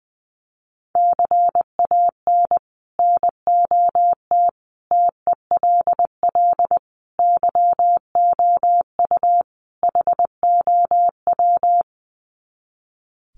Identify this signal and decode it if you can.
Morse code, heard as a keyed tone, about 19 wpm, 700 Hz. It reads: CAN NOT TELL YOU HOW
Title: Sets of 5 Words Encore2 20wpm